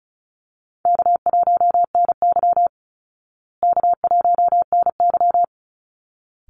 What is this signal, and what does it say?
Morse code, heard as a keyed tone, about 34 wpm, 700 Hz. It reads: K1NY K1NY
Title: Easy Call Signs Repeated 2x - 35wpm